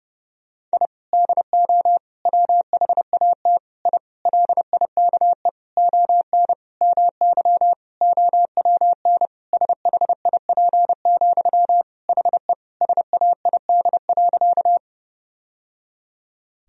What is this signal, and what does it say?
Morse code, heard as a keyed tone, about 30 wpm, 700 Hz. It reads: I DO WHAT I LIKE ON MY OWN SHIP, HE SAID.
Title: Sentences from Top 1000 Words - Review Part 2 - 30wpm